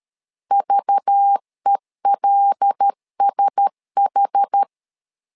Morse code, heard as a keyed tone, about 13 wpm, 800 Hz. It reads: VELSH
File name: LEZIONI / 5 LBVC / 5 LBVC 12wpm 1955